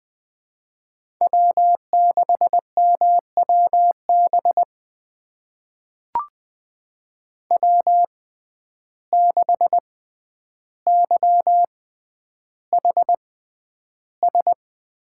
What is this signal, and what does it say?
Morse code, heard as a keyed tone, about 21 wpm, spaced as for 10 wpm, 700 Hz. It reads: W6MWB W6YHS
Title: USA Call Signs - 5 Characters - 6x Character Spacing - 20wpm